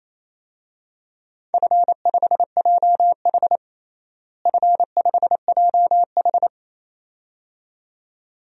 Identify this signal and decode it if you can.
Morse code, heard as a keyed tone, about 27 wpm, 700 Hz. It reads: F5JH F5JH
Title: DXCC-100 Call Signs - 4 Characters - Repeated 2x - 28wpm